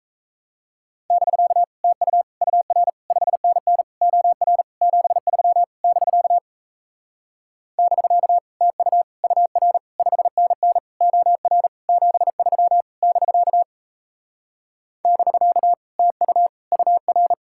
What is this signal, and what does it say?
Morse code, heard as a keyed tone, about 37 wpm, 700 Hz. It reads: <BK> TU UR 5NN OR 73 <BK> <BK> TU UR 5NN OR 73 <BK> <BK> TU UR